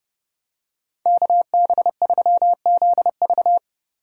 Morse code, heard as a keyed tone, about 31 wpm, 700 Hz. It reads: KB3ZV